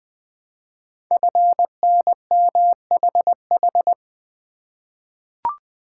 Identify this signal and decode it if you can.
Morse code, heard as a keyed tone, about 20 wpm, 700 Hz. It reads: FNMHH
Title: Code Groups - 5 Letters - 20wpm